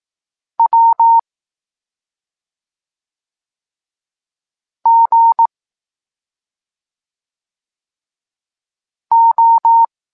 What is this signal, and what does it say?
Morse code, heard as a keyed tone, about 18 wpm, 920 Hz. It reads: W G O
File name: Tekst-23 4